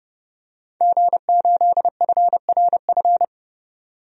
Morse code, heard as a keyed tone, about 31 wpm, 700 Hz. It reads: G8FRF
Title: DXCC-100 Call Signs - 5 Characters - 30wpm